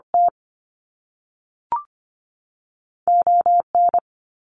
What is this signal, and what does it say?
Morse code, heard as a keyed tone, about 25 wpm, 700 Hz. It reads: T ON
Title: Canadian Provinces and Territories 25wpm